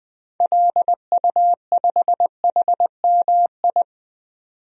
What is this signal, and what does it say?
Morse code, heard as a keyed tone, about 20 wpm, 700 Hz. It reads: LU5HMI
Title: DXCC-100 Call Signs - 6 Characters - 20wpm